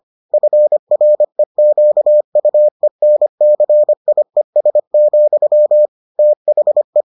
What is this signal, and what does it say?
Morse code, heard as a keyed tone, about 25 wpm, 600 Hz. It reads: FREQUENCIES, THE